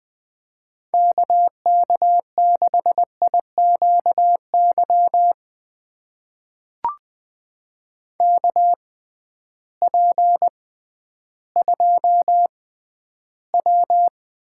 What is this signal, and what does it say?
Morse code, heard as a keyed tone, about 21 wpm, spaced as for 12 wpm, 700 Hz. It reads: KK6IQY KP2W
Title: USA Call Signs - 6x Character Spacing - 20wpm